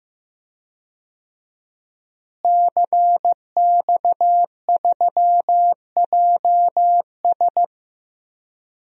Morse code, heard as a keyed tone, about 15 wpm, 700 Hz. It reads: CX3JS
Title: DXCC-100 Call Signs - 5 Characters - Repeated 2x - 15wpm